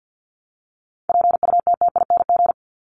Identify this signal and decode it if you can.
Morse code, heard as a keyed tone, about 55 wpm, 700 Hz. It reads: PUTTING
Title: ING Suffix - Top 100 Words with Base Words 50wpm